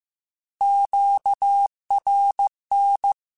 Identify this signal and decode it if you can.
Morse code, heard as a keyed tone, about 15 wpm, 800 Hz. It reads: QRN
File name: Short Runs / Short Runs-mp3s / Words @13